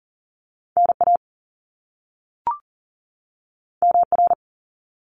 Morse code, heard as a keyed tone, about 41 wpm, 700 Hz. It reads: NA MR
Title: Top Two Letter Combinations in Words 40wpm